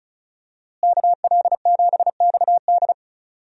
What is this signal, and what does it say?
Morse code, heard as a keyed tone, about 34 wpm, 700 Hz. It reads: KL7XD
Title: DXCC-100 Call Signs - 5 Characters - Repeated 3x - 35wpm